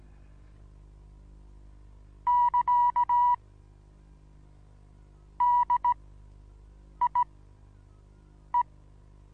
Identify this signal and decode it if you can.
Morse code, heard as a keyed tone, about 16 wpm, spaced as for 6 wpm, 970 Hz. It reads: <KA> DIE
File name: Onlinelehrgang / morsen / DJ4UF Morsen25